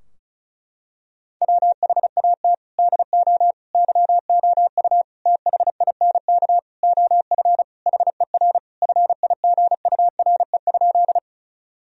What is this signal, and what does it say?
Morse code, heard as a keyed tone, about 36 wpm, 700 Hz. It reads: WHAT DO YOU THINK OF HER FIGURE?
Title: Sentences from Top 1000 Words - Review Part 2 - 35wpm